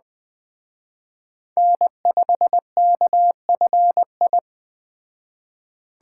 Morse code, heard as a keyed tone, about 20 wpm, 700 Hz. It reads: N5KFI
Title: Speed Racing - USA 1x3 Call Signs 17wpm